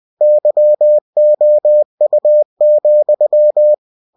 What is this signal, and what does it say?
Morse code, heard as a keyed tone, about 20 wpm, 600 Hz. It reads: YOU,
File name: cw-books / winnie-the-pooh / winnie-20wpm-page-0005